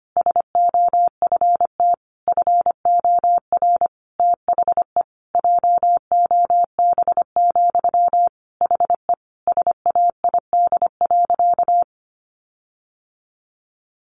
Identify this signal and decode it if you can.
Morse code, heard as a keyed tone, about 25 wpm, 700 Hz. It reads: SOFT FOR THE JOB, HE SAID.
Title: Sentences from Top 1500 Words 25wpm